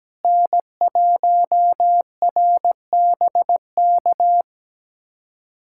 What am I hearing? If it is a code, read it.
Morse code, heard as a keyed tone, about 17 wpm, 700 Hz. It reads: N1RBK
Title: USA Callsign DE Callsign - 5 Characters - 17wpm